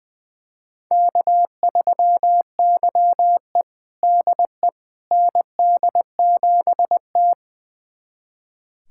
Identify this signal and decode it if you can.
Morse code, heard as a keyed tone, about 20 wpm, 700 Hz. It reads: K3YE DE ND7T
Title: USA Callsign DE Callsign - 4 Characters - 20wpm